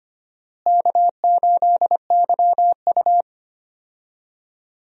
Morse code, heard as a keyed tone, about 25 wpm, 700 Hz. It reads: K8YU
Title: USA Call Signs - 4 Characters - Repeated 3x - 25wpm